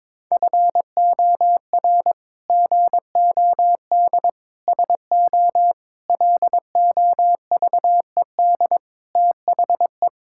Morse code, heard as a keyed tone, about 22 wpm, 700 Hz. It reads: FOR GOD SO LOVED THE
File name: practice / 22wpm / Sets of 8 Words - 22wpm